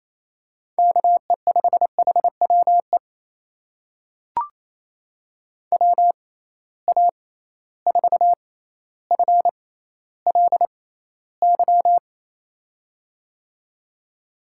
Morse code, heard as a keyed tone, about 28 wpm, spaced as for 14 wpm, 700 Hz. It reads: KE5HWE WA4FLY
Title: USA Call Signs - 6 Characters - 6x Character Spacing - 28wpm